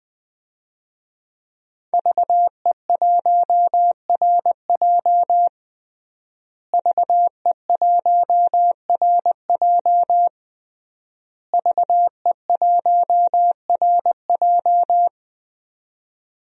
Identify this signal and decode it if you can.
Morse code, heard as a keyed tone, about 21 wpm, 700 Hz. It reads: VE1RJ VE1RJ VE1RJ
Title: DXCC-100 Call Signs - 5 Characters - Repeated 3x - 20wpm